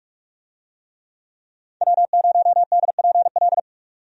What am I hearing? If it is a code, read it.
Morse code, heard as a keyed tone, about 47 wpm, 700 Hz. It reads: W0DPL
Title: USA Call Signs - 5 Characters - 45wpm